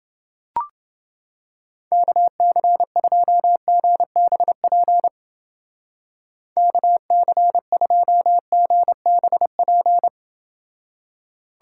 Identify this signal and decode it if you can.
Morse code, heard as a keyed tone, about 29 wpm, 700 Hz. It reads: KC2GBP KC2GBP